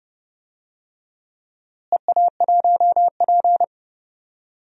Morse code, heard as a keyed tone, about 32 wpm, 700 Hz. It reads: EA1P
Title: DXCC-100 Call Signs - 4 Characters - Repeated 3x - 30wpm